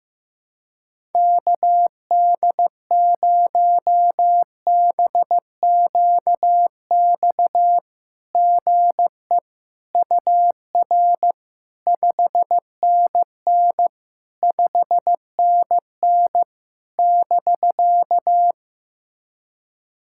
Morse code, heard as a keyed tone, about 15 wpm, 700 Hz. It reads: KD0BQX GE UR 5NN 5NN <BK>
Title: SOTA QSO Practice 15wpm